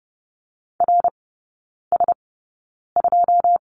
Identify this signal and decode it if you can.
Morse code, heard as a keyed tone, about 30 wpm, spaced as for 9 wpm, 700 Hz. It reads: RS2
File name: Lesson 18e - Letter m - Three Characters 30wpm